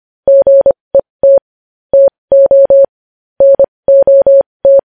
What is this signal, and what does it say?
Morse code, heard as a keyed tone, about 25 wpm, spaced as for 20 wpm, 550 Hz. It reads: GET TO NOT
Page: News Headlines in Morse Code at 20 WPM